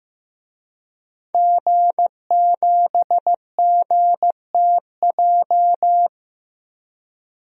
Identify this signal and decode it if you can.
Morse code, heard as a keyed tone, about 15 wpm, 700 Hz. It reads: G7GTJ
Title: DXCC-100 Call Signs - 5 Characters - 15wpm